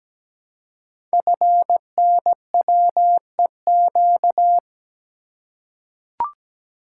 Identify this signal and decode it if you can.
Morse code, heard as a keyed tone, about 17 wpm, 700 Hz. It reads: FNWEQ
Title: Code Groups - 5 Letters - 17wpm